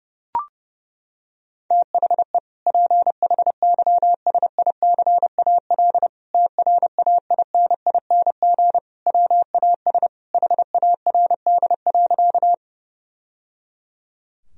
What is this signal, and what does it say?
Morse code, heard as a keyed tone, about 30 wpm, 700 Hz. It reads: THE PHYSICAL TRAINING WAS HARD.
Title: Sentences from Top 1000 Words - Review Part 4 - 30wpm